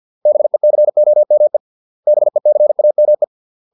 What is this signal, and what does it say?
Morse code, heard as a keyed tone, about 50 wpm, 600 Hz. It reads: BECOME BECAME